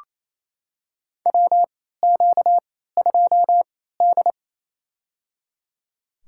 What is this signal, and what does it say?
Morse code, heard as a keyed tone, about 27 wpm, spaced as for 16 wpm, 700 Hz. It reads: WQ2D
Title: USA Call Signs - 4 Characters - 3x Character Spacing - 28wpm